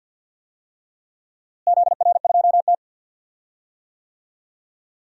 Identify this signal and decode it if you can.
Morse code, heard as a keyed tone, about 48 wpm, 700 Hz. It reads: ZR2T
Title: DXCC-100 Call Signs - 4 Characters - Repeated 3x - 50wpm